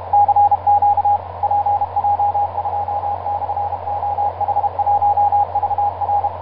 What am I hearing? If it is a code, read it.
Morse code, heard as a keyed tone, about 36 wpm, 820 Hz, with radio noise. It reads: CQ PJ2/OH1VR
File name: pj2-oh1vr iz2amw etc 12m cw 24mar2010 1955